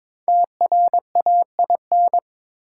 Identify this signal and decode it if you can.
Morse code, heard as a keyed tone, about 22 wpm, 700 Hz. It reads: TRAIN
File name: 5 Letter Words - First 3 Letters Given - 22wpm